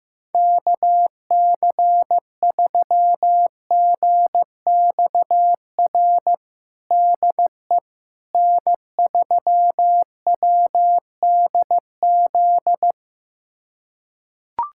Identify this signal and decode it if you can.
Morse code, heard as a keyed tone, about 15 wpm, 700 Hz. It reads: KC3GXR DE N3WDZ
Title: USA Callsign DE Callsign 15wpm